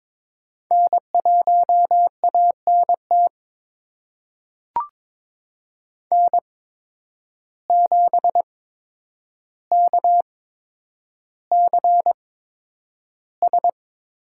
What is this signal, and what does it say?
Morse code, heard as a keyed tone, about 22 wpm, spaced as for 9 wpm, 700 Hz. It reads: N1ANT N7KCS
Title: USA Call Signs - 5 Characters - 8x Character Spacing - 22wpm